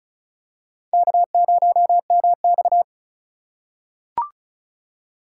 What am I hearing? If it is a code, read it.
Morse code, heard as a keyed tone, about 34 wpm, 700 Hz. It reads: K0MX
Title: USA Call Signs - 3x Character Spacing - 35wpm